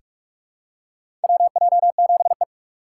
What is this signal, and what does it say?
Morse code, heard as a keyed tone, about 43 wpm, 690 Hz. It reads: WJ7E
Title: USA Call Signs - 4 Characters - 3x Character Spacing - 45wpm